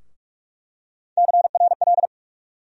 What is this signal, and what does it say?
Morse code, heard as a keyed tone, about 43 wpm, 700 Hz. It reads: CRL
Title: USA Call Sign Postfixes - 3 Letters 45wpm